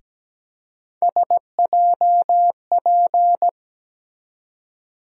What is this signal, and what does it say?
Morse code, heard as a keyed tone, about 17 wpm, 700 Hz. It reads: SJP
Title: USA Call Sign Postfixes - 3 Letters 17wpm